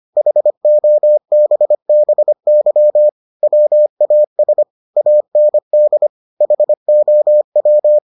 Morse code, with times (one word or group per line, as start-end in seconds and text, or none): 0.2-3.1 HOBBY
3.4-4.6 WAS
5.0-6.1 AND
6.4-8.0 HOW